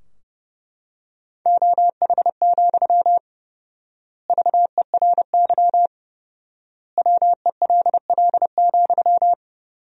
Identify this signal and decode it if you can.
Morse code, heard as a keyed tone, about 30 wpm, 700 Hz. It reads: OH, VERY WELL,
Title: 4x Word-Spacing - Sentences from Top 500 Words - 30wpm